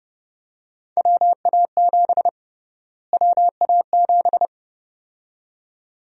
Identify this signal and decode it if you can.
Morse code, heard as a keyed tone, about 32 wpm, 700 Hz. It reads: WA7 WA7
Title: USA Call Sign Prefixes - Repeated 2x - 30wpm